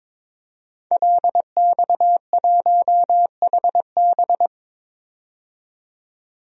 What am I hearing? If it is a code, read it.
Morse code, heard as a keyed tone, about 22 wpm, 700 Hz. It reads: LX1HB
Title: DXCC-100 Call Signs - 5 Characters - 22wpm